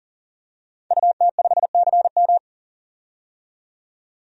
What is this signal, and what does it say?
Morse code, heard as a keyed tone, about 40 wpm, 700 Hz. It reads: UT5CM
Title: DXCC-100 Call Signs - 5 Characters - 40wpm